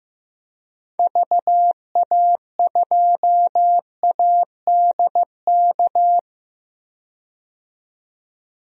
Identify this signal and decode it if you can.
Morse code, heard as a keyed tone, about 15 wpm, 700 Hz. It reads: VA2ADK
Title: DXCC-100 Call Signs - 6 Characters - 15wpm